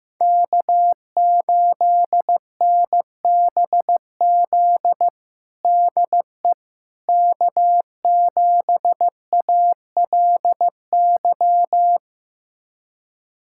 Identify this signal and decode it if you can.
Morse code, heard as a keyed tone, about 15 wpm, 700 Hz. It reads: K8NBZ DE K7ALY
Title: USA Callsign DE Callsign - 5 Characters - 15wpm